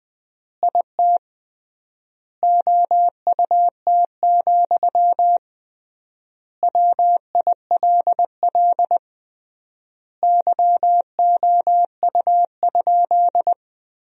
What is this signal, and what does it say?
Morse code, heard as a keyed tone, about 20 wpm, 700 Hz. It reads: IT OUT, WILL YOU?
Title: 3x Word-Spacing - Sentences from Top 300 Words - 20wpm